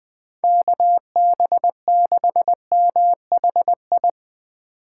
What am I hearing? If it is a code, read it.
Morse code, heard as a keyed tone, about 22 wpm, 700 Hz. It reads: KB6MHI
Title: Speed Racing - USA 2x3 Call Signs 20wpm